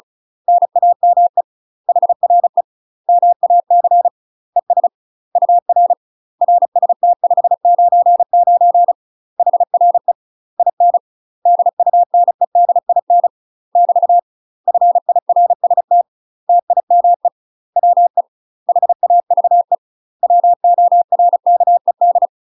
Morse code, heard as a keyed tone, about 33 wpm, 700 Hz. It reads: NAME HRE MAC ES UR RST 599 HRE IN DUNEDIN <BT> FIRST TIME WE HAVE WORKED
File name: UK MASTER QSOs TRAINING / 35WPM / UK MASTER QSOs 0018